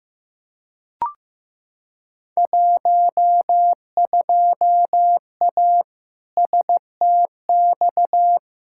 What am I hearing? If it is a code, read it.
Morse code, heard as a keyed tone, about 15 wpm, 700 Hz. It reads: 12A STX
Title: Field Day Exchanges Encore - 15wpm